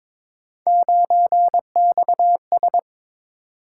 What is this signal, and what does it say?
Morse code, heard as a keyed tone, about 22 wpm, 700 Hz. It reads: 9XS
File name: practice / 22wpm / Three Letter-Number Combinations - 22wpm